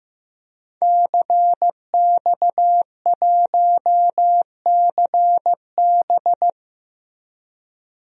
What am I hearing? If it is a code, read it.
Morse code, heard as a keyed tone, about 16 wpm, 700 Hz. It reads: CX1CB
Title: DXCC-100 Call Signs - 5 Characters - Repeated 3x - 15wpm